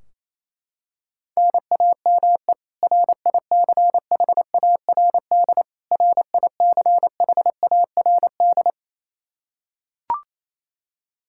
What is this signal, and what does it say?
Morse code, heard as a keyed tone, about 28 wpm, 700 Hz. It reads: NAME RICHARD RICHARD
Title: Practice QSO - Exchange 1 - Beginning - 28wpm